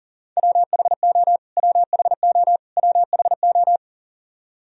Morse code, heard as a keyed tone, about 40 wpm, 700 Hz. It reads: WHO WHO WHO
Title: Top 100 Words Repeated 40wpm